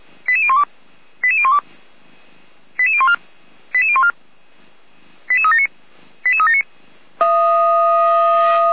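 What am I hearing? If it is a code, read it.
Five-tone selective call (ZVEI), heard as quick runs of five tones, tones of about 70 ms. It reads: 88012 88012 88014 88014 88379 88379
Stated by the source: ZVEI five-tone selective call 88199 received on an SDR by Feuerwehr Orlamuende.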